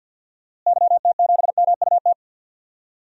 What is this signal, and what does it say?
Morse code, heard as a keyed tone, about 47 wpm, 700 Hz. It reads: YT7MUT
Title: DXCC-100 Call Signs - 6 Characters - Repeated 3x - 50wpm